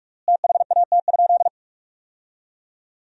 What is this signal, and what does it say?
Morse code, heard as a keyed tone, about 46 wpm, 700 Hz. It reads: THAT?
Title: 3x Word-Spacing - Sentences from Top 100 Words - 45wpm